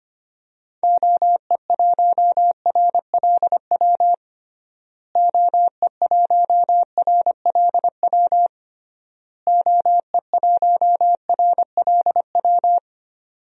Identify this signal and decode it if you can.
Morse code, heard as a keyed tone, about 25 wpm, 700 Hz. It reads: OE1RLW OE1RLW OE1RLW
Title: DXCC-100 Call Signs - 6 Characters - Repeated 3x - 25wpm